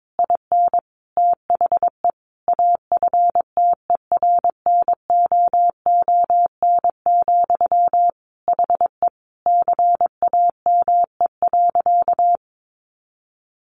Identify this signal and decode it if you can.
Morse code, heard as a keyed tone, about 22 wpm, 700 Hz. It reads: IN THE AFTERNOON, HE CAME.
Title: Sentences from Top 1500 Words 22wpm